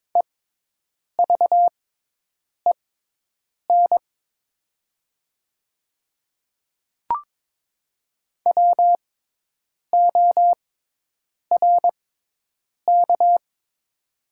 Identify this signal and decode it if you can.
Morse code, heard as a keyed tone, about 22 wpm, spaced as for 8 wpm, 700 Hz. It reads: EVEN WORK
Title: 6x Character Spacing - Top 300 Words - 22wpm